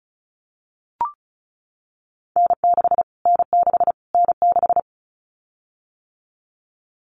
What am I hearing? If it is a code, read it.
Morse code, heard as a keyed tone, about 34 wpm, 700 Hz. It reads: N6 N6 N6
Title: Callsign Prefixes for Top 25 DXCC - Repeated 3x - 35wpm